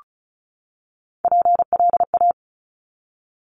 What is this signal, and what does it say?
Morse code, heard as a keyed tone, about 36 wpm, 700 Hz. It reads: PLA
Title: Top Three Letter Combinations in Words 35wpm